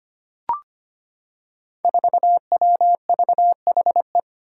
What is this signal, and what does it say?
Morse code, heard as a keyed tone, about 25 wpm, 700 Hz. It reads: 4WVHE
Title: Five Challenging Letter-Number Combinations 25wpm